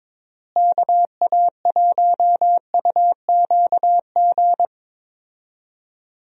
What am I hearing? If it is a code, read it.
Morse code, heard as a keyed tone, about 21 wpm, 700 Hz. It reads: KA1UQG
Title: USA Call Signs - 6 Characters - Repeated 3x - 22wpm